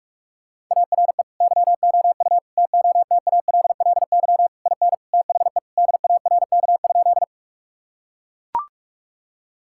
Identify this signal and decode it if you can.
Morse code, heard as a keyed tone, about 42 wpm, 700 Hz. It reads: ARE YOU TOTALLY IN THE DARK?
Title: Sentences from Top 1600 Words 45wpm